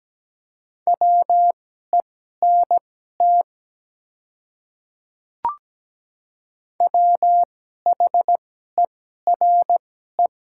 Morse code, heard as a keyed tone, about 17 wpm, spaced as for 12 wpm, 700 Hz. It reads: WENT WHERE
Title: 2x Character Spacing - Top 300 Words - 17wpm